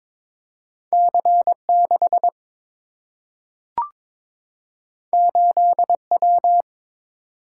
Morse code, heard as a keyed tone, about 22 wpm, 700 Hz. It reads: C6 8W
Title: Single-Letter Single-Number 22wpm